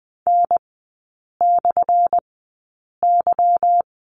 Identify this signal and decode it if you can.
Morse code, heard as a keyed tone, about 20 wpm, spaced as for 8 wpm, 700 Hz. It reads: N/Y
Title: 178 Lesson 29e - Slash - Three Characters 20wpm